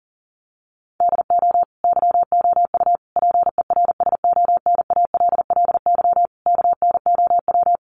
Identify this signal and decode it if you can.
Morse code, heard as a keyed tone, about 40 wpm, 700 Hz. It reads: DO YOU PERSONALLY KNOW
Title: Sentences from Top 1400 Words 40wpm